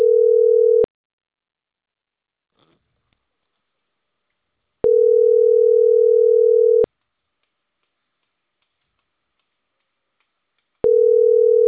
Ringback tone (North America), 440 + 480 Hz.